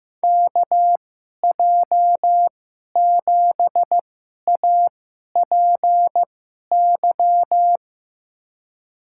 Morse code, heard as a keyed tone, about 15 wpm, spaced as for 11 wpm, 700 Hz. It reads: KJ7APY
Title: USA Call Signs - 6 Characters - 2x Character Spacing - 15wpm